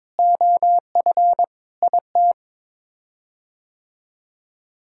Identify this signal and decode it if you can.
Morse code, heard as a keyed tone, about 22 wpm, 700 Hz. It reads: OF IT
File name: practice / 22wpm / Sets of 7 Words - 22wpm